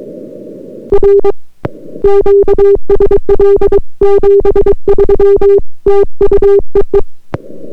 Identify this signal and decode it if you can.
Morse code, heard as a keyed tone, about 22 wpm, 400 Hz. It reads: R QSL 73 TU EE